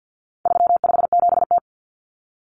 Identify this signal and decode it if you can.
Morse code, heard as a keyed tone, about 50 wpm, 700 Hz. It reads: 357T
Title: Common Cut Numbers - 4-Digit 50wpm